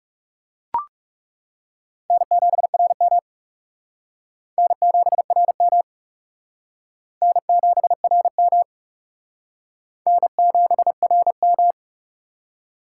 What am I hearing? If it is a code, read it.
Morse code, heard as a keyed tone, about 36 wpm, 700 Hz. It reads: N7RM N7RM N7RM N7RM
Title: Speed Racing - USA 1x2 Call Signs 30wpm